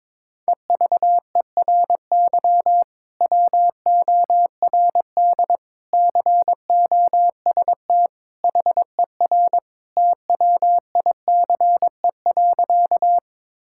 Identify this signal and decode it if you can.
Morse code, heard as a keyed tone, about 22 wpm, 700 Hz. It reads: EVERY WORD COST HER TWICE.
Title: Sentences from Top 1600 Words 22wpm